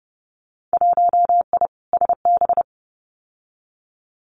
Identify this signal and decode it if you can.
Morse code, heard as a keyed tone, about 30 wpm, 700 Hz. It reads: 1I SB
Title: Winter Field Day Exchanges 30wpm